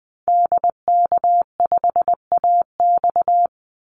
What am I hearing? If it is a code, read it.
Morse code, heard as a keyed tone, about 20 wpm, 700 Hz. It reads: DK5AX
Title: Call Signs Repeated 3x - 20wpm